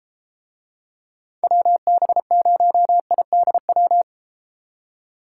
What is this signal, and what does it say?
Morse code, heard as a keyed tone, about 32 wpm, 700 Hz. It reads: WB0IDW